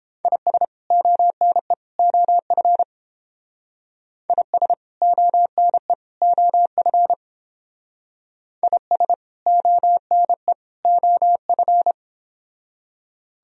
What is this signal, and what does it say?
Morse code, heard as a keyed tone, about 29 wpm, 700 Hz. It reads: IS ONE OF IS ONE OF IS ONE OF